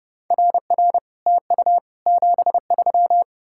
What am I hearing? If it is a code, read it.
Morse code, heard as a keyed tone, about 40 wpm, 700 Hz. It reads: RR TU 73